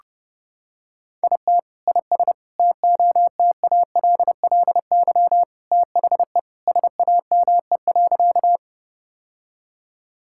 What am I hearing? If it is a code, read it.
Morse code, heard as a keyed tone, about 32 wpm, 700 Hz. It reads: IT IS TOTALLY THE SAME.
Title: Sentences from Top 1600 Words 30wpm